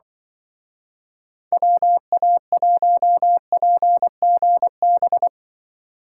Morse code, heard as a keyed tone, about 24 wpm, 700 Hz. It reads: WA1PGB
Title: Speed Racing - USA 2x3 Call Signs 20wpm